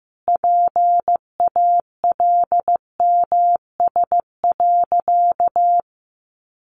Morse code, heard as a keyed tone, about 15 wpm, 700 Hz. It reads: PALMS.
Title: A Sound Of Thunder 15wpm